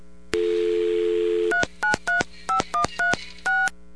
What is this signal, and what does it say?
DTMF, heard as telephone touch tones, tones of about 120 ms. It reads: 3935136